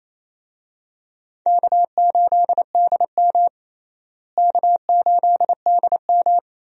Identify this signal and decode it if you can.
Morse code, heard as a keyed tone, about 27 wpm, 700 Hz. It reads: K8DM K8DM
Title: USA Call Signs - 4 Characters - Repeated 2x - 28wpm